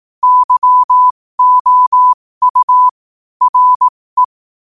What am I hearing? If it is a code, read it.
Morse code, heard as a keyed tone, about 18 wpm, spaced as for 15 wpm, 1000 Hz. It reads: YOU RE
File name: qotdfiles / 15wpm